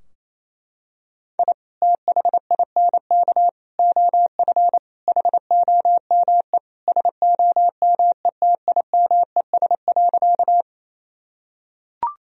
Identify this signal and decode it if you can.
Morse code, heard as a keyed tone, about 28 wpm, 700 Hz. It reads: I THINK OF HOME SOMETIMES.
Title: Sentences from Top 1000 Words - Review Part 3 - 28wpm